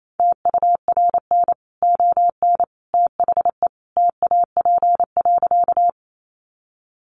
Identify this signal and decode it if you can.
Morse code, heard as a keyed tone, about 28 wpm, 700 Hz. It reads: TURN ON THE TAP.
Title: Sentences from Top 1400 Words 28wpm